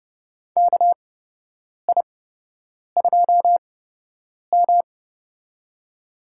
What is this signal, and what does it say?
Morse code, heard as a keyed tone, about 30 wpm, spaced as for 8 wpm, 700 Hz. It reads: KI2M
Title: USA Call Signs - 4 Characters - 8x Character Spacing - 30wpm